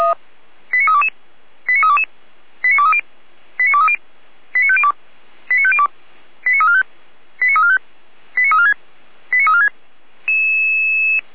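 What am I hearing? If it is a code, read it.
Five-tone selective call (ZVEI), heard as quick runs of five tones, tones of about 70 ms. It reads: 89322 89322 89230 89230 89502 89502 89345 89345 89346 89346
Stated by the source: ZVEI five-tone selective call received on SDR, from feuerwehr-orlamuende.org (fire brigade Orlamünde).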